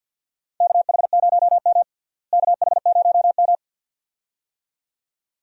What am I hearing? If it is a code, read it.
Morse code, heard as a keyed tone, about 47 wpm, 700 Hz. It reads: KH0M KH0M
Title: USA Call Signs - 4 Characters - Repeated 2x - 50wpm